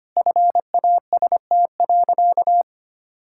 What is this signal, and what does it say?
Morse code, heard as a keyed tone, about 25 wpm, 700 Hz. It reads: FAST.
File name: practice / 25wpm / Sentences with 2-Word Phrases - 25wpm